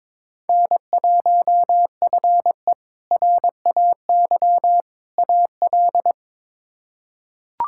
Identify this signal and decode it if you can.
Morse code, heard as a keyed tone, about 22 wpm, 700 Hz. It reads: N1FE RAY AL
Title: K1USN SST QSO Practice - 22wpm